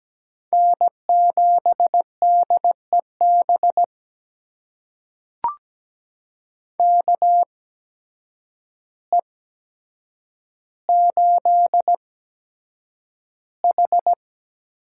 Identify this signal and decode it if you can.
Morse code, heard as a keyed tone, about 17 wpm, spaced as for 8 wpm, 700 Hz. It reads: N7DEB KE8H
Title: USA Call Signs - 8x Character Spacing - 17wpm